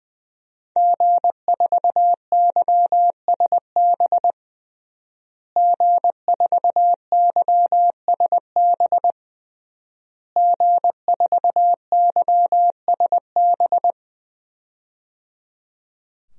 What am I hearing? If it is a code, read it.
Morse code, heard as a keyed tone, about 20 wpm, 700 Hz. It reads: G4YSB G4YSB G4YSB
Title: DXCC-100 Call Signs - 5 Characters - Repeated 3x - 20wpm